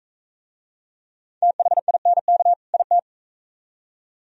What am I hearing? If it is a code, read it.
Morse code, heard as a keyed tone, about 40 wpm, 700 Hz. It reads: THINK IT